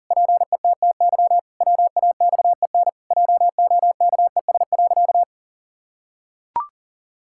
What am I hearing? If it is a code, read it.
Morse code, heard as a keyed tone, about 40 wpm, 700 Hz. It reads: PETTY WAXEN JOKES.
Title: Pangram Challenge v2 - 40wpm